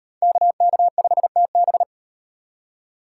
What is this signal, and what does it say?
Morse code, heard as a keyed tone, about 37 wpm, 700 Hz. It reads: KK5TB